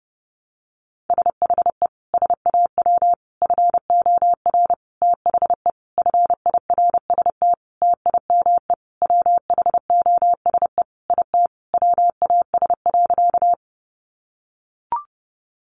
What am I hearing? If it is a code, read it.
Morse code, heard as a keyed tone, about 30 wpm, 700 Hz. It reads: SHE SAW FOR THE FIRST TIME WHOSE IT WAS.
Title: Sentences from Top 1200 Words 30wpm